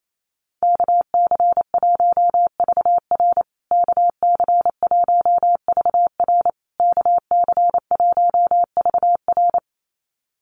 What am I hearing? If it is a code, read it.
Morse code, heard as a keyed tone, about 28 wpm, 700 Hz. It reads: KC1VR KC1VR KC1VR
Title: USA Call Signs Repeated 3x 28wpm